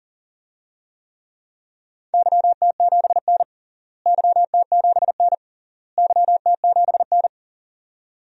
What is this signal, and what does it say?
Morse code, heard as a keyed tone, about 37 wpm, 700 Hz. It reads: YT7N YT7N YT7N
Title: DXCC-100 Call Signs - 4 Characters - Repeated 3x - 40wpm